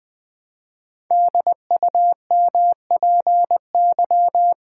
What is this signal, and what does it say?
Morse code, heard as a keyed tone, about 20 wpm, 700 Hz. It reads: DUMPY
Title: Pangram Challenge v2 - 20wpm